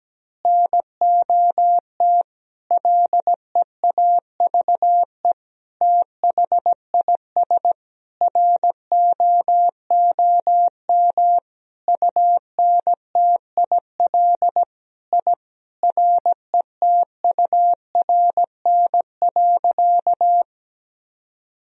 Morse code, heard as a keyed tone, about 17 wpm, 700 Hz. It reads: NOT LEAVE THIS ROOM UNTIL I RETURN.
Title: Sentences from Top 1000 Words - Review Part 1 - 17wpm